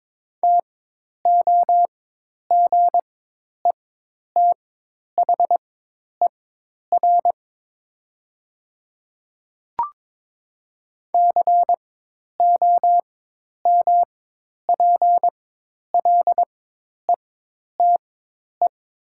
Morse code, heard as a keyed tone, about 22 wpm, spaced as for 10 wpm, 700 Hz. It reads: TOGETHER COMPLETE
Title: 4x Character Spacing - Top 500 Words - 22wpm